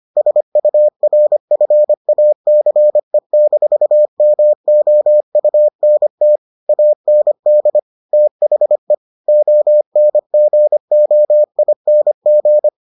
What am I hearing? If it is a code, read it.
Morse code, heard as a keyed tone, about 25 wpm, 600 Hz. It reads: SURFACE-MOUNT AND THE ONGOING